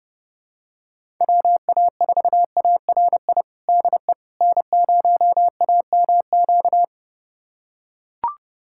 Morse code, heard as a keyed tone, about 30 wpm, 700 Hz. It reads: WA4ARI DE N0AMQ
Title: USA Callsign DE Callsign 30wpm